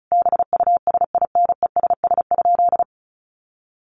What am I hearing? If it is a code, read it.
Morse code, heard as a keyed tone, about 34 wpm, 700 Hz. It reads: BUSINESS?
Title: Sentences from Top 1100 Words 35wpm